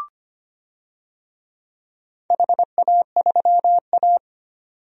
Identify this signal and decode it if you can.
Morse code, heard as a keyed tone, about 25 wpm, 700 Hz. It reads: HA3A
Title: DXCC-100 Call Signs - 4 Characters - 25wpm